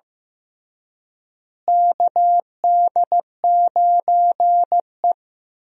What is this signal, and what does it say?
Morse code, heard as a keyed tone, about 15 wpm, 700 Hz. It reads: KD9E
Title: USA Call Signs - 4 Characters - Repeated 3x - 15wpm